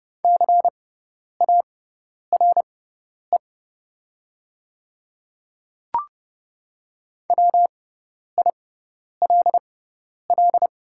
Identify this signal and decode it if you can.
Morse code, heard as a keyed tone, about 30 wpm, spaced as for 10 wpm, 700 Hz. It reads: CARE WILL
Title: 6x Character Spacing - Top 500 Words - 30wpm